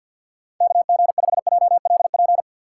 Morse code, heard as a keyed tone, about 52 wpm, 700 Hz. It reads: KG5JLP
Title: USA Call Signs - 6 Characters - 4x Character Spacing - 50wpm